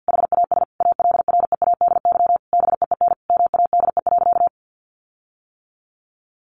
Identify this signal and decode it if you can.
Morse code, heard as a keyed tone, about 50 wpm, 700 Hz. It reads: HAS ALREADY BEEN MADE.
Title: Sentences from Top 1400 Words 50wpm